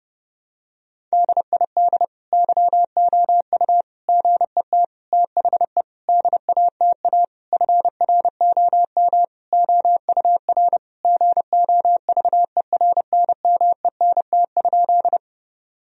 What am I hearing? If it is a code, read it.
Morse code, heard as a keyed tone, about 30 wpm, 700 Hz. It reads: DID YOU GET THE DATA FROM OUR GOVERNMENT?
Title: Sentences from Top 1000 Words - Review Part 3 - 30wpm